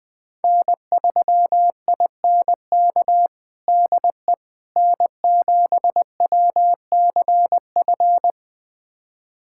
Morse code, heard as a keyed tone, about 19 wpm, 700 Hz. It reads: N3INK DE N7WCF
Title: USA Callsign DE Callsign - 5 Characters - 20wpm